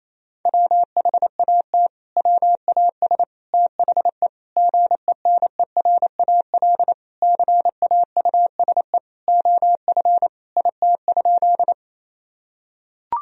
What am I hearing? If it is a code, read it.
Morse code, heard as a keyed tone, about 28 wpm, 700 Hz. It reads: WHAT WAS THE GENERAL CAUSE OF IT?
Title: Sentences from Top 1000 Words - Review Part 3 - 28wpm